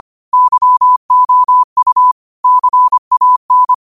Morse code, heard as a keyed tone, about 25 wpm, 1000 Hz. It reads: YOU CAN
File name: qotdfiles / 25wpm